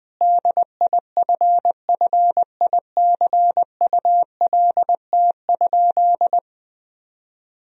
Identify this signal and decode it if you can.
Morse code, heard as a keyed tone, about 21 wpm, 700 Hz. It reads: DIFFICULT?
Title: Sentences from Top 1000 Words - Review Part 4 - 20wpm